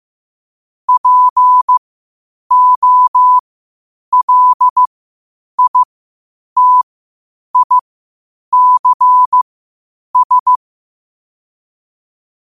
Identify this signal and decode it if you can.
Morse code, heard as a keyed tone, about 15 wpm, spaced as for 8 wpm, 1000 Hz. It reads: POLITICS
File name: qotdfiles / 8wpm